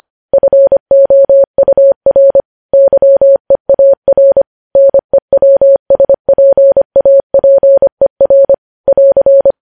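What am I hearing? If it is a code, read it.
Morse code, heard as a keyed tone, about 25 wpm, 550 Hz. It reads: FOUR YEAR NEWSPAPER <AR>
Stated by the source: Morse news headlines sent at 25 wpm.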